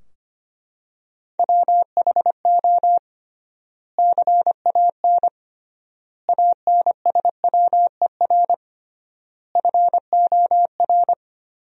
Morse code, heard as a keyed tone, about 25 wpm, 700 Hz. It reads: WHO CAN ANSWER FOR
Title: 3x Word-Spacing - Sentences from Top 300 Words - 25wpm